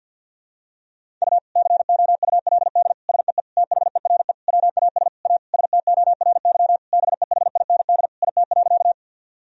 Morse code, heard as a keyed tone, about 52 wpm, 700 Hz. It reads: U COULD SEE THERE WAS A STORY BEHIND IT.
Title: Sentences from Top 500 Words and Abbreviations 50wpm